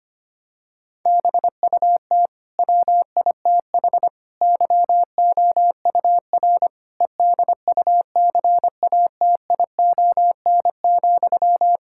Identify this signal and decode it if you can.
Morse code, heard as a keyed tone, about 25 wpm, 700 Hz. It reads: BUT WITH YOUR EDUCATION,
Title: Sentences from Top 1000 Words - Review Part 6 - 25wpm